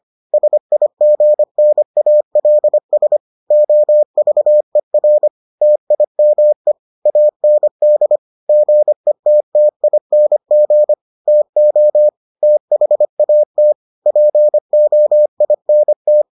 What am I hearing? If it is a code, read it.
Morse code, heard as a keyed tone, about 25 wpm, 600 Hz. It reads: SIGNALS OVER TIME AND GETTING TO THAT POINT